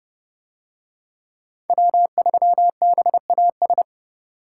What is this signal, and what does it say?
Morse code, heard as a keyed tone, about 28 wpm, 700 Hz. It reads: W3BAS